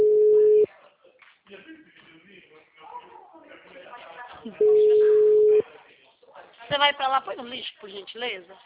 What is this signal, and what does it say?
Ringback tone (Europe, 425 Hz), with speech.